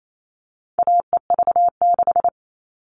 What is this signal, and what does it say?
Morse code, heard as a keyed tone, about 28 wpm, 700 Hz. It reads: AEV6
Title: All Cut Numbers - 4-Digit 28wpm